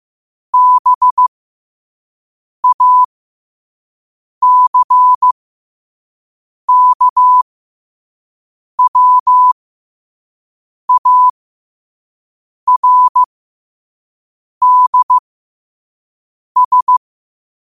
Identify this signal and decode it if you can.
Morse code, heard as a keyed tone, about 15 wpm, spaced as for 5 wpm, 1000 Hz. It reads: BACKWARDS
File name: 5wpm